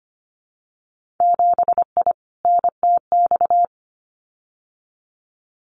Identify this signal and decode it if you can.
Morse code, heard as a keyed tone, about 25 wpm, 700 Hz. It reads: 7I NTX
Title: Winter Field Day Exchanges Encore 25wpm